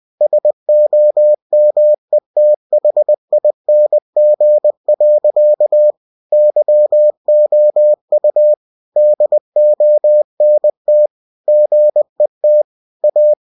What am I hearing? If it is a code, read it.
Morse code, heard as a keyed tone, about 20 wpm, 600 Hz. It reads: SOMETHING. YOU DONT GET A
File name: cw-books / winnie-the-pooh / winnie-20wpm-page-0002